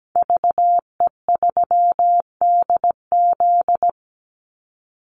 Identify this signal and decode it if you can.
Morse code, heard as a keyed tone, about 17 wpm, 700 Hz. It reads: VE3DZ
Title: Call Signs Repeated 3x - 17wpm